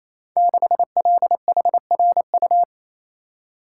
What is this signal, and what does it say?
Morse code, heard as a keyed tone, about 28 wpm, 700 Hz. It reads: 6LHRU
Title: Code Groups - 5 Letter:Numbers - 28wpm